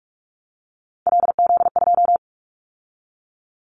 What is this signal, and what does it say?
Morse code, heard as a keyed tone, about 46 wpm, 700 Hz. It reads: LZ2
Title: Callsign Prefixes for Top 50 DXCC - 45wpm